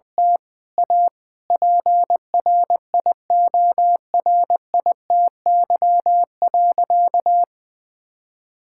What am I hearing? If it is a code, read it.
Morse code, heard as a keyed tone, about 21 wpm, 700 Hz. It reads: T A PRIORITY.
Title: Sentences from Top 1600 Words 20wpm